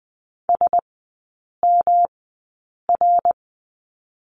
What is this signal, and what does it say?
Morse code, heard as a keyed tone, about 20 wpm, spaced as for 8 wpm, 690 Hz. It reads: SMR
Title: 145 Lesson 24e - Letter y - Three Characters 20wpm